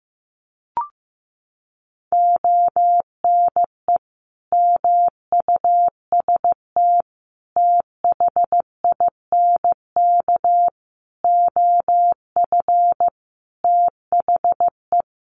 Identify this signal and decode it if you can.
Morse code, heard as a keyed tone, about 15 wpm, 700 Hz. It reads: ONE MUST THINK OF THE
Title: Sentences from Top 700 Words 15wpm